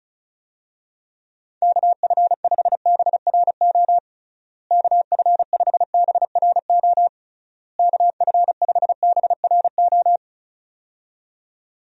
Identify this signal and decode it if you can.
Morse code, heard as a keyed tone, about 34 wpm, 700 Hz. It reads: KF5BRO KF5BRO KF5BRO
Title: USA Call Signs - 6 Characters - Repeated 3x - 35wpm